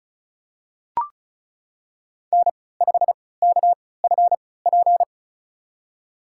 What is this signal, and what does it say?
Morse code, heard as a keyed tone, about 34 wpm, spaced as for 20 wpm, 700 Hz. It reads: N5KFP
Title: USA Call Signs - 5 Characters - 3x Character Spacing - 35wpm